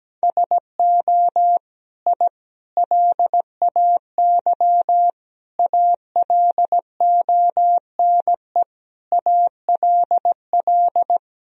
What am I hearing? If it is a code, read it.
Morse code, heard as a keyed tone, about 17 wpm, 700 Hz. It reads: SO I LAY ALONE ALL
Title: Sentences from Top 1000 Words - Review Part 2 - 17wpm